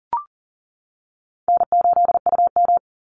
Morse code, heard as a keyed tone, about 40 wpm, 700 Hz. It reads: N8UM
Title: Easy Call Signs - 40wpm